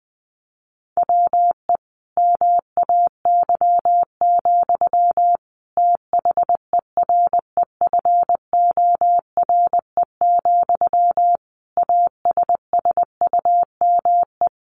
Morse code, heard as a keyed tone, about 20 wpm, 700 Hz. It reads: WE MAY, THEREFORE, ASSUME
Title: Sentences from Top 1500 Words 20wpm